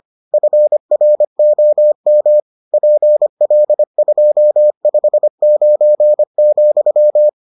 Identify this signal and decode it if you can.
Morse code, heard as a keyed tone, about 25 wpm, 600 Hz. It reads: FROM PL259,